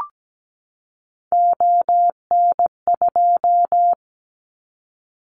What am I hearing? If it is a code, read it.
Morse code, heard as a keyed tone, about 17 wpm, 700 Hz. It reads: ON2
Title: Callsign Prefixes for Top 50 DXCC - 17wpm